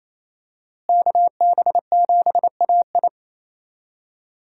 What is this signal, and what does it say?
Morse code, heard as a keyed tone, about 28 wpm, 700 Hz. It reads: KB7AI
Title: USA Call Signs - 5 Characters - Repeated 3x - 28wpm